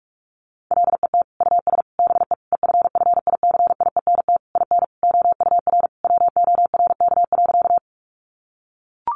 Morse code, heard as a keyed tone, about 44 wpm, 700 Hz. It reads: LET US BE EFFICIENT IN OUR WORK.
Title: Sentences from Top 1300 Words 45wpm